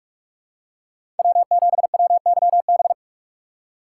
Morse code, heard as a keyed tone, about 48 wpm, 700 Hz. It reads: W7WYB
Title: USA Call Signs - 5 Characters - 3x Character Spacing - 45wpm